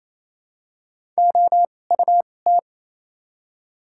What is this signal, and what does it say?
Morse code, heard as a keyed tone, about 28 wpm, spaced as for 20 wpm, 700 Hz. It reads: OUT
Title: 2x Character Spacing - Top 100 Words - 28wpm